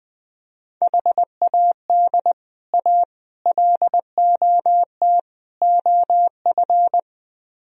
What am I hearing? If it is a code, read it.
Morse code, heard as a keyed tone, about 21 wpm, 700 Hz. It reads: HAD A LOT OF
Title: Speed-Racing - Sets of 4 Words 20wpm